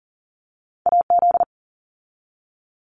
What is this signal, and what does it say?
Morse code, heard as a keyed tone, about 40 wpm, 700 Hz. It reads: AZ
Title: ARRL Sections 40wpm